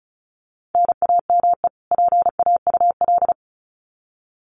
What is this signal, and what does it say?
Morse code, heard as a keyed tone, about 34 wpm, 700 Hz. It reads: NAME PAUL
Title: Common QSO Phrases 35wpm